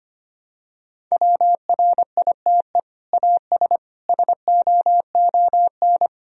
Morse code, heard as a keyed tone, about 25 wpm, 700 Hz. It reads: WRITE AS SOON
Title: Sentences from Top 1000 Words - Review Part 1 - 25wpm